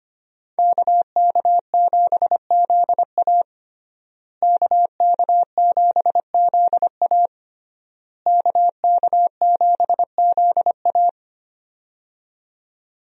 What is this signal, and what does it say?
Morse code, heard as a keyed tone, about 25 wpm, 700 Hz. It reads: KK7ZA KK7ZA KK7ZA
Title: USA Call Signs - 5 Characters - Repeated 3x - 25wpm